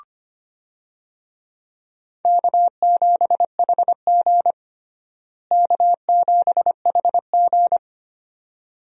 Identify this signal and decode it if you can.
Morse code, heard as a keyed tone, about 25 wpm, 700 Hz. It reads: K7HG K7HG
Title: USA Call Signs - 4 Characters - Repeated 2x - 25wpm